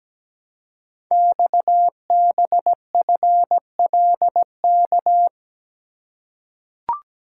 Morse code, heard as a keyed tone, about 17 wpm, 700 Hz. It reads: XBFLK
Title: Code Groups - 5 Letters - 17wpm